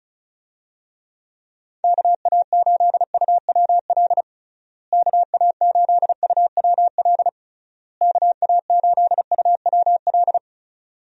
Morse code, heard as a keyed tone, about 36 wpm, 700 Hz. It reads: KA8UWL KA8UWL KA8UWL
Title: USA Call Signs - 6 Characters - Repeated 3x - 35wpm